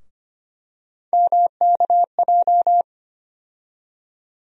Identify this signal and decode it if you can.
Morse code, heard as a keyed tone, about 25 wpm, 700 Hz. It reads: MKJ
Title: USA Call Sign Postfixes - 3 Letters 25wpm